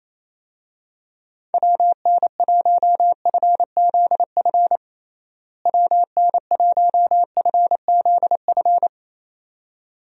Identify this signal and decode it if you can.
Morse code, heard as a keyed tone, about 28 wpm, 700 Hz. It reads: WN1FZF WN1FZF
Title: USA Call Signs - 6 Characters - Repeated 2x - 28wpm